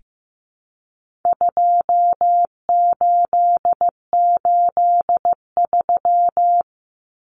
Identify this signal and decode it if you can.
Morse code, heard as a keyed tone, about 15 wpm, 700 Hz. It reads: 2883
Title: Common Cut Numbers - 4-Digit 15wpm